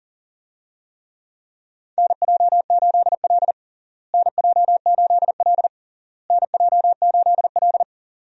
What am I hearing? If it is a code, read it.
Morse code, heard as a keyed tone, about 40 wpm, 700 Hz. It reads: NJ8L NJ8L NJ8L
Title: USA Call Signs - 4 Characters - Repeated 3x - 40wpm